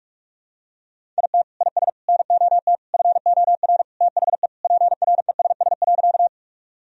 Morse code, heard as a keyed tone, about 45 wpm, 700 Hz. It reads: IT IS NOT FOR THE PRESS.
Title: Sentences from Top 1000 Words - Review Part 1 - 45wpm